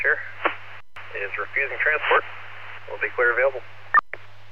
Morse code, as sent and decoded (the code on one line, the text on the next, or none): none